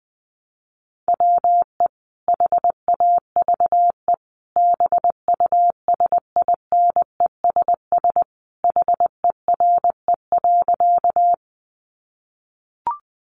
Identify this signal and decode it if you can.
Morse code, heard as a keyed tone, about 19 wpm, 700 Hz. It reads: WE HAVE BUSINESS HERE.
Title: Sentences from Top 600 Words 20wpm